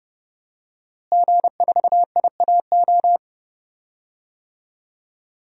Morse code, heard as a keyed tone, about 32 wpm, 700 Hz. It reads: G4IAO
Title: DXCC-100 Call Signs - 5 Characters - Repeated 2x - 30wpm